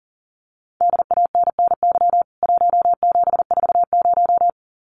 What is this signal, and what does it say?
Morse code, heard as a keyed tone, about 40 wpm, 700 Hz. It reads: DANNY 1740
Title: CWT Exchange 40wpm